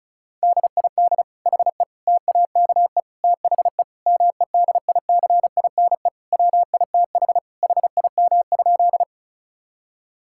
Morse code, heard as a keyed tone, about 34 wpm, 700 Hz. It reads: DID HE TAKE THE MEDICINE WITH HIM?
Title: Sentences from Top 1000 Words - Review Part 6 - 35wpm